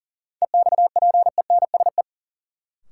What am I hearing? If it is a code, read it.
Morse code, heard as a keyed tone, about 40 wpm, 700 Hz. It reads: EXPENSE
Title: Sets of 4 Words - New Word Drill - 40wpm